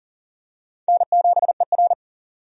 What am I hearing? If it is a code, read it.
Morse code, heard as a keyed tone, about 40 wpm, 700 Hz. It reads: N7ER
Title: USA Call Signs - 4 Characters - 40wpm